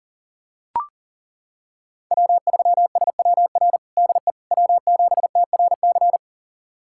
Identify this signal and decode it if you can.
Morse code, heard as a keyed tone, about 40 wpm, 700 Hz. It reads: W3SWR DE W7TRC
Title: USA Callsign DE Callsign - 5 Characters - 40wpm